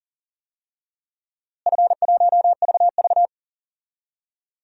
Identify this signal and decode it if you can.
Morse code, heard as a keyed tone, about 40 wpm, 700 Hz. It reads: F1VV
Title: DXCC-100 Call Signs - 4 Characters - Repeated 3x - 40wpm